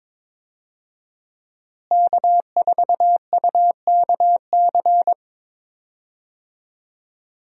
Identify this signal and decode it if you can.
Morse code, heard as a keyed tone, about 22 wpm, 700 Hz. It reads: K4UKC
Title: USA Call Signs - 5 Characters - Repeated 3x - 22wpm